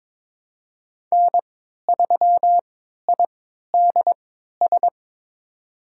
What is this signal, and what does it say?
Morse code, heard as a keyed tone, about 22 wpm, spaced as for 13 wpm, 700 Hz. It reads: N3IDS
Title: USA Call Signs - 3x Character Spacing - 22wpm